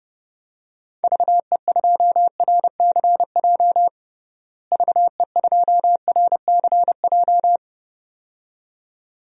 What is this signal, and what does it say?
Morse code, heard as a keyed tone, about 30 wpm, 700 Hz. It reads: VE2RCJ VE2RCJ
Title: DXCC-100 Call Signs - 6 Characters - Repeated 2x - 30wpm